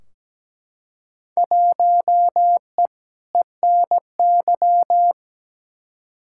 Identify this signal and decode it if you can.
Morse code, heard as a keyed tone, about 17 wpm, 700 Hz. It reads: 1E ENY
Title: Field Day Exchanges - 17wpm